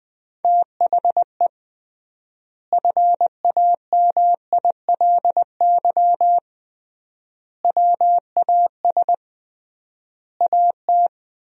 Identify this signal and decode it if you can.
Morse code, heard as a keyed tone, about 20 wpm, 700 Hz. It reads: THE FAMILY WAS AT
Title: 3x Word-Spacing - Sentences from Top 400 Words - 20wpm